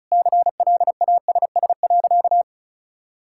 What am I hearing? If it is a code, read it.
Morse code, heard as a keyed tone, about 36 wpm, 700 Hz. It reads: CLASS.
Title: Sentences from Top 1000 Words - Review Part 5 - 35wpm